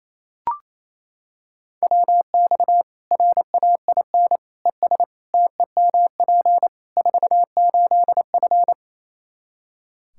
Morse code, heard as a keyed tone, about 28 wpm, 700 Hz. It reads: WX RAIN ES TEMP 48F
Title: Practice QSO - Exchange 2 - Info - 28wpm